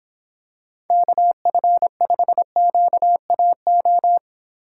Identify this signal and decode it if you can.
Morse code, heard as a keyed tone, about 26 wpm, 700 Hz. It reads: KF5QAO